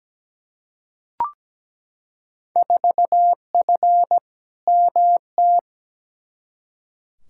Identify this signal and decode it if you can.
Morse code, heard as a keyed tone, about 17 wpm, 700 Hz. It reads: 4F MT
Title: Field Day Exchanges Encore - 17wpm